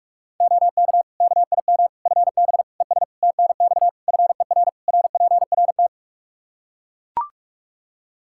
Morse code, heard as a keyed tone, about 41 wpm, 700 Hz. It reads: OK KIM FB ES TNX FER RPRT
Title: Practice QSO - Exchange 2 - Info - 45wpm